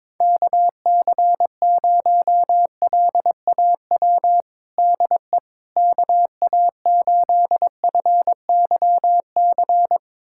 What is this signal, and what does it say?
Morse code, heard as a keyed tone, about 22 wpm, 700 Hz. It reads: KC0LAW DE KA8FYC
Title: USA Callsign DE Callsign - 6 Characters - 22wpm